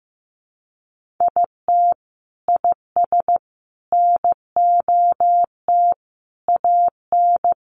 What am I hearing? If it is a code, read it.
Morse code, heard as a keyed tone, about 15 wpm, 700 Hz. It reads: IT IS NOT AN
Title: Sentences from Top 1500 Words 15wpm